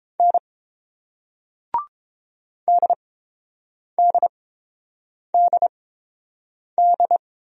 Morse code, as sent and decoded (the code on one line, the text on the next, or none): -. / -.. -.. -.. -..
N DDDD